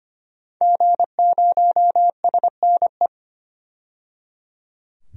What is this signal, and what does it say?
Morse code, heard as a keyed tone, about 25 wpm, 700 Hz. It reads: G0SNE
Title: DXCC-100 Call Signs - 5 Characters - 25wpm